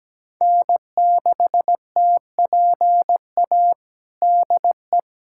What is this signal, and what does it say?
Morse code, heard as a keyed tone, about 17 wpm, 700 Hz. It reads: N6TPA DE
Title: USA Callsign DE Callsign - 5 Characters - 17wpm